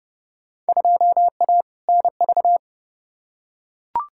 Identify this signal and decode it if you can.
Morse code, heard as a keyed tone, about 30 wpm, 700 Hz. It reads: 2A NV
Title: Field Day Exchanges - 30wpm